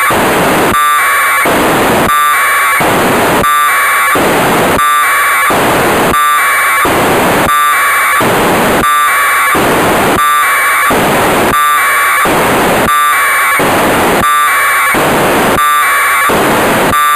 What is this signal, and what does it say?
Packet radio (AX.25, as in APRS).
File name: Negai 1200bd packet received by JA6PL provided by JA0CAW